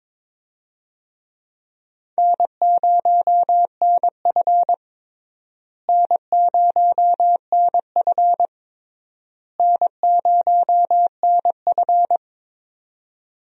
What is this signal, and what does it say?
Morse code, heard as a keyed tone, about 22 wpm, 700 Hz. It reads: N0NF N0NF N0NF
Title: USA Call Signs - 4 Characters - Repeated 3x - 22wpm